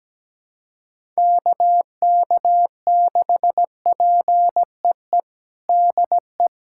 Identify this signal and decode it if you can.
Morse code, heard as a keyed tone, about 17 wpm, 700 Hz. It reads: KK6PEE DE
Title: USA Callsign DE Callsign - 6 Characters - 17wpm